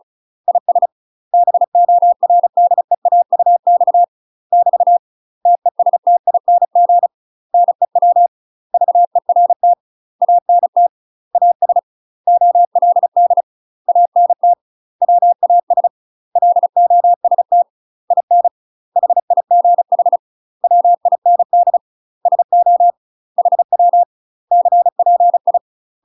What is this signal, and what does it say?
Morse code, heard as a keyed tone, about 36 wpm, 700 Hz. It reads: IS BORDEAUX <BT> TESTING NEW VERT ANT AS OLD ANT WAS LOST IN HIGH WIND SO HW CPI
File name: UK MASTER QSOs TRAINING / 35WPM / UK MASTER QSOs 0026